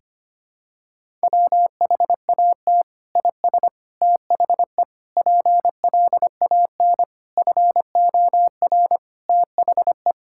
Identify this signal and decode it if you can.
Morse code, heard as a keyed tone, about 25 wpm, 700 Hz. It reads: WHAT IS THE PLAN FOR THE
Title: Sentences from Top 1000 Words - Review Part 5 - 25wpm